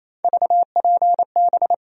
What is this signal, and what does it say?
Morse code, heard as a keyed tone, about 28 wpm, 700 Hz. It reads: VPB
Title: USA Call Sign Postfixes - 3 Letters 28wpm